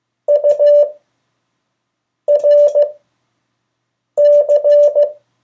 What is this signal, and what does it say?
Morse code, heard as a keyed tone, about 15 wpm, 580 Hz. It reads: U R C
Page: CW Class Review – Delaware Valley Radio Association